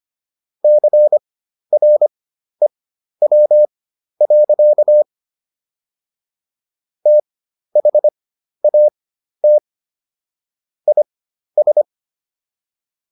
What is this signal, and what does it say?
Morse code, heard as a keyed tone, about 25 wpm, spaced as for 12 wpm, 600 Hz. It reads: CREW. THAT IS